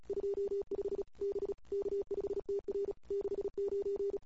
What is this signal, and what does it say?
Morse code, heard as a keyed tone, about 36 wpm, 400 Hz. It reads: 25 B K5TR 69
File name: CW / 2008-ss-cw-k5tr-op-n5rz -21:00